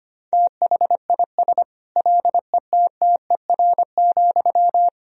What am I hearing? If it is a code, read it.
Morse code, heard as a keyed tone, about 25 wpm, 700 Hz. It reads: THIS LETTER,
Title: Sentences from Top 1000 Words - Review Part 4 - 25wpm